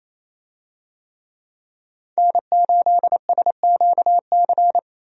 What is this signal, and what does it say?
Morse code, heard as a keyed tone, about 28 wpm, 700 Hz. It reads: N8SQC
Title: USA Call Signs - 5 Characters - 28wpm